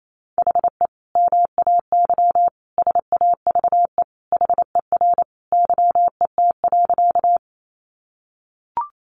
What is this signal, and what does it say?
Morse code, heard as a keyed tone, about 28 wpm, 700 Hz. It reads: HE MAY SAVE HER YET.
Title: Sentences from Top 1100 Words 28wpm